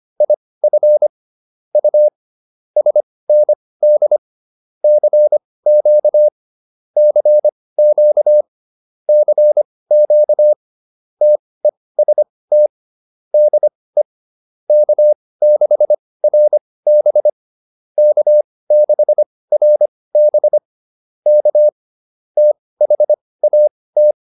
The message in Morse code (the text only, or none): IF U SND CQ CQ CQ TEST DE K6RB K6RB K THAT